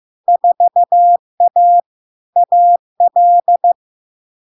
Morse code, heard as a keyed tone, about 15 wpm, 700 Hz. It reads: 4A AL
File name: LARK-FD-2022-Log-in-CW-15wpm-700-Hz